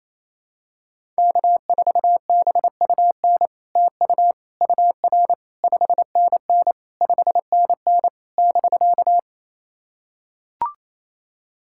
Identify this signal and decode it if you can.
Morse code, heard as a keyed tone, about 28 wpm, 700 Hz. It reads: K4BUN TU UR 5NN 5NN <BK>
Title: SOTA QSO Practice 28wpm